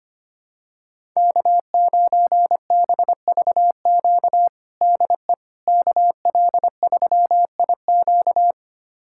Morse code, heard as a keyed tone, about 25 wpm, 700 Hz. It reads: K9BVQ DE KL3IQ
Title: USA Callsign DE Callsign - 5 Characters - 25wpm